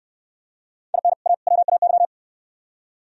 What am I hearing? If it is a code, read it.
Morse code, heard as a keyed tone, about 61 wpm, 700 Hz. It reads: IN A WAY